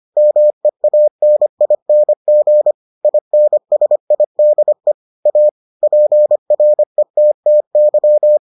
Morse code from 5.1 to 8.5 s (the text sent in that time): A PRETTY